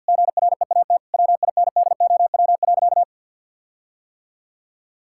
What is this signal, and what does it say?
Morse code, heard as a keyed tone, about 50 wpm, 700 Hz. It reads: GREAT WINDOW.
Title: Sentences from Top 1600 Words 50wpm